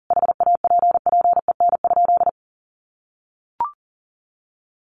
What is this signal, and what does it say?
Morse code, heard as a keyed tone, about 40 wpm, 700 Hz. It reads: HAPPEN?
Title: Sentences from Top 400 Words 40wpm